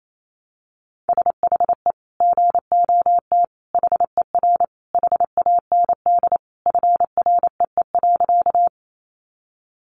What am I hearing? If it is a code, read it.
Morse code, heard as a keyed tone, about 28 wpm, 700 Hz. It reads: SHE GOT HER HAND FREE.
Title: Sentences from Top 500 Words 28wpm